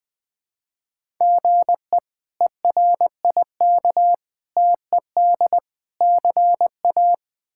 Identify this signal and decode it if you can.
Morse code, heard as a keyed tone, about 20 wpm, 700 Hz. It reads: GE ERIK TED CA
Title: K1USN SST QSO Practice - 20wpm